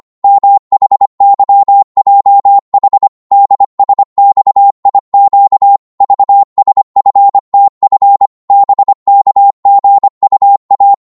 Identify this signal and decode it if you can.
Morse code, heard as a keyed tone, about 25 wpm, 800 Hz. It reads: MHYJH DSXIQ VSFTF BKGUA
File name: Morse / ltr-25wpm